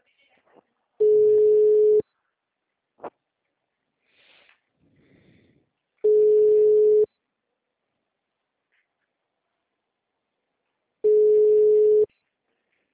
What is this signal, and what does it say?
Ringback tone (Europe, 425 Hz), with speech.